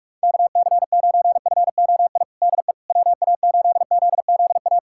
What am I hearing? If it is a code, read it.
Morse code, heard as a keyed tone, about 41 wpm, 700 Hz. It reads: KC9FOI DE WA8ZZA
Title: USA Callsign DE Callsign - 6 Characters - 45wpm